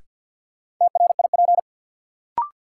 Morse code, heard as a keyed tone, about 53 wpm, 700 Hz. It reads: TRIP
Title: Sentences from Top 1300 Words - New Word Drill Alt - 50wpm